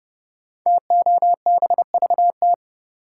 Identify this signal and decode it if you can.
Morse code, heard as a keyed tone, about 30 wpm, 700 Hz. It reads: TOBVT
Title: Code Groups - 5 Letters - 30wpm